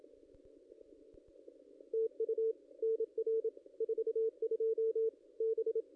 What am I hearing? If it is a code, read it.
Morse code, heard as a keyed tone, about 27 wpm, 450 Hz. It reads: TU NR 42 B
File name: contests / ss-cw-k5tr-op-n5rz -1600